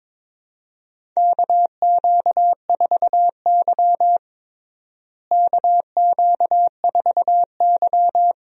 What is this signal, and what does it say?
Morse code, heard as a keyed tone, about 22 wpm, 700 Hz. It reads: KQ4Y KQ4Y
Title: USA Call Signs - 4 Characters - Repeated 2x - 22wpm